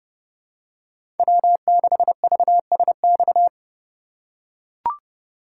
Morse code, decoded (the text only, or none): W6VSX